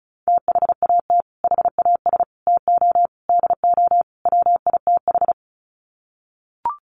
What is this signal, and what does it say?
Morse code, heard as a keyed tone, about 34 wpm, 700 Hz. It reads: THAT HAS TO DO WITH
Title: Sets of 5 Words Encore2 35wpm